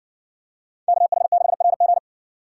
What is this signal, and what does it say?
Morse code, heard as a keyed tone, about 60 wpm, 700 Hz. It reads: KH6RG